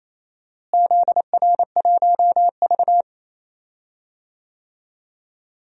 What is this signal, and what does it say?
Morse code, heard as a keyed tone, about 27 wpm, 700 Hz. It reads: ZR1V
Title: DXCC-100 Call Signs - 4 Characters - Repeated 3x - 28wpm